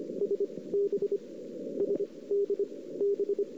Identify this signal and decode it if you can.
Morse code, heard as a keyed tone, about 26 wpm, spaced as for 19 wpm, 400 Hz. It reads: S B SD B
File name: CW / 2008-ss-cw-k5tr-op-n5rz -05:00